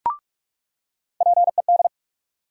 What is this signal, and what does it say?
Morse code, heard as a keyed tone, about 46 wpm, 700 Hz. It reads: PED
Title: USA Call Sign Postfixes - 3 Letters 45wpm